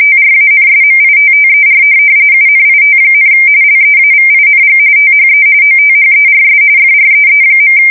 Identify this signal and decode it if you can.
RTTY, about 45.45 baud. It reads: TRANSMITTER UNIT, LEAVING THE OTHERS FREE.
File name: fsk170 / morkrum